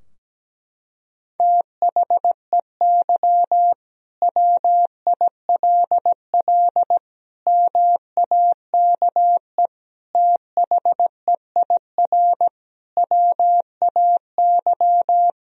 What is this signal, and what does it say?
Morse code, heard as a keyed tone, about 17 wpm, 700 Hz. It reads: THEY WILL MAKE THEIR WAY
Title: Sentences from Top 1000 Words - Review Part 3 - 17wpm